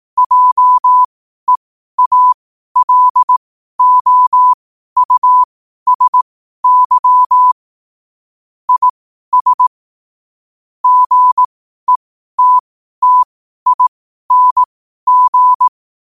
Morse code, heard as a keyed tone, about 18 wpm, spaced as for 12 wpm, 1000 Hz. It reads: JEALOUSY IS GETTING
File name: qotdfiles / 12wpm